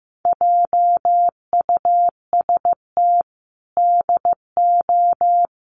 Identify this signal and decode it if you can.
Morse code, heard as a keyed tone, about 15 wpm, 700 Hz. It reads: JUST DO
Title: Sets of 5 Words Encore 15wpm